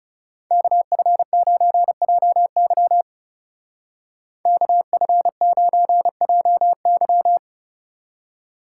Morse code, heard as a keyed tone, about 36 wpm, 700 Hz. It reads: KF9JY KF9JY
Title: Speed Racing - USA 2x2 Call Signs 30wpm